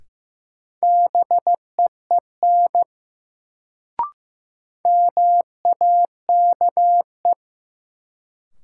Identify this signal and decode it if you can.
Morse code, heard as a keyed tone, about 15 wpm, 700 Hz. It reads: BEEN MAKE
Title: Sentences from Top 100 Words - New Word Drill - 15wpm